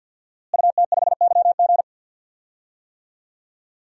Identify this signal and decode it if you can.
Morse code, heard as a keyed tone, about 50 wpm, 700 Hz. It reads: UT5YG
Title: DXCC-100 Call Signs - 5 Characters - Repeated 3x - 50wpm